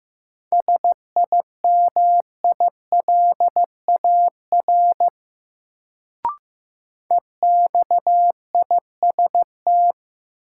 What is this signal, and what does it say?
Morse code, heard as a keyed tone, about 15 wpm, 700 Hz. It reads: SIMILAR EXIST
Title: Sentences from Top 1000 Words - Review Part 5 - New Word Drill - 15wpm